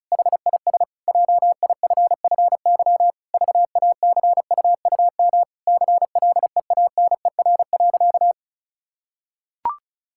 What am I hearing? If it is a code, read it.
Morse code, heard as a keyed tone, about 34 wpm, 700 Hz. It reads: HIS JIFFY VACUUM CLEANER.
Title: Pangram Challenge v2 - 35wpm